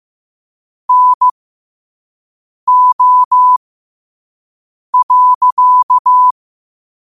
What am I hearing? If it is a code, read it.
Morse code, heard as a keyed tone, about 15 wpm, spaced as for 5 wpm, 1000 Hz. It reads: NO.
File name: 5wpm